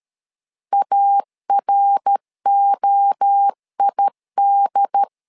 Morse code, heard as a keyed tone, about 13 wpm, 800 Hz. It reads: AROID
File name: LEZIONI / 3 MWRO / 3 MWRO 12wpm 1730